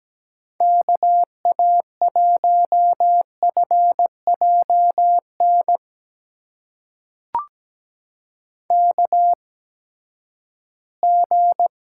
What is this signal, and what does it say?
Morse code, heard as a keyed tone, about 17 wpm, spaced as for 12 wpm, 700 Hz. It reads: KA1FJN KG
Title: USA Call Signs - 6 Characters - 8x Character Spacing - 17wpm